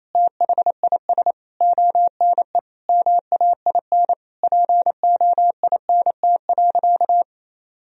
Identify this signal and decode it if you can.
Morse code, heard as a keyed tone, about 28 wpm, 700 Hz. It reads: THIS ONE MAIN POINT.
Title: Sentences from Top 1000 Words - Review Part 2 - 28wpm